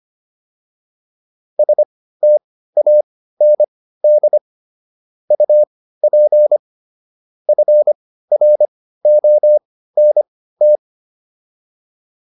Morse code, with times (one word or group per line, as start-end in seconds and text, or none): 1.6-4.4 STAND
5.3-6.6 UP
7.5-10.8 FRONT